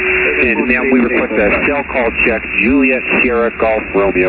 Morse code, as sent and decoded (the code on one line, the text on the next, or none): none